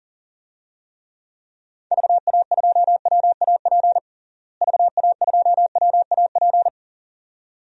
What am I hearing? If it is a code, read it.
Morse code, heard as a keyed tone, about 40 wpm, 700 Hz. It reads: VA2WAP VA2WAP
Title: DXCC-100 Call Signs - 6 Characters - Repeated 2x - 40wpm